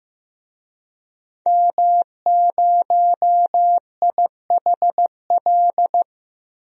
Morse code, heard as a keyed tone, about 15 wpm, 700 Hz. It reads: M0IHL
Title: DXCC-100 Call Signs - 5 Characters - Repeated 3x - 15wpm